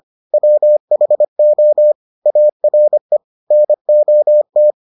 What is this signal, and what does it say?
Morse code, heard as a keyed tone, about 25 wpm, 600 Hz. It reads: WHO ARE NOT